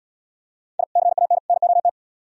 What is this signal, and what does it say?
Morse code, heard as a keyed tone, about 77 wpm, 700 Hz. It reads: I CAN NOT